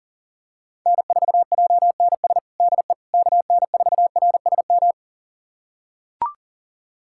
Morse code, heard as a keyed tone, about 40 wpm, 700 Hz. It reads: N4JNS DE KN4RSM
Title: USA Callsign DE Callsign 40wpm